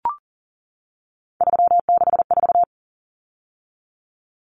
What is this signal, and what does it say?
Morse code, heard as a keyed tone, about 40 wpm, 700 Hz. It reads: 364
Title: Common Cut Numbers - 3-Digit 40wpm